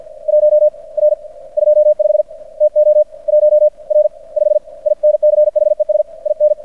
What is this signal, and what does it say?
Morse code, heard as a keyed tone, about 50 wpm, spaced as for 35 wpm, 600 Hz. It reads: 1R25 TO 1R5 INCHES IN
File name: archived / CRSnet